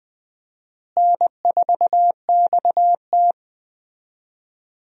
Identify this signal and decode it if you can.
Morse code, heard as a keyed tone, about 20 wpm, 700 Hz. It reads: N4XT
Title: USA Call Signs - 4 Characters - Repeated 2x - 20wpm